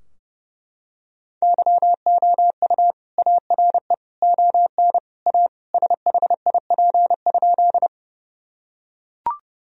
Morse code, heard as a keyed tone, about 30 wpm, 700 Hz. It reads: YOU ARE ON A SHIP?
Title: Sentences from Top 1000 Words - Review Part 2 - 30wpm